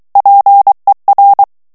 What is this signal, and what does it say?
Morse code, heard as a keyed tone, about 23 wpm, 800 Hz. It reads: PER
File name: morse mp3 / ITALIEN texte clair 1 vitesse 22 mots minute